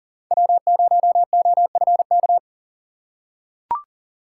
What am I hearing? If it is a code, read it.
Morse code, heard as a keyed tone, about 40 wpm, 700 Hz. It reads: W0OFK
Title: USA Call Signs - 5 Characters - 2x Character Spacing - 40wpm